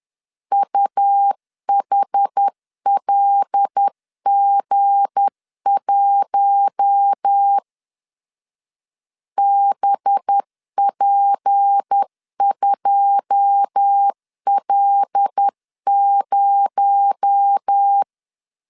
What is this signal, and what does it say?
Morse code, heard as a keyed tone, about 10 wpm, 800 Hz. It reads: UHLG1 BP2L0
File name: LEZIONI / 9 286 10wpm 2735